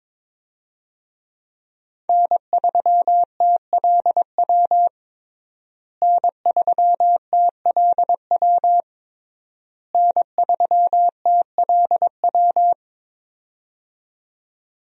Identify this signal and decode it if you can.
Morse code, heard as a keyed tone, about 22 wpm, 700 Hz. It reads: N3TLW N3TLW N3TLW
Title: USA Call Signs - 5 Characters - Repeated 3x - 22wpm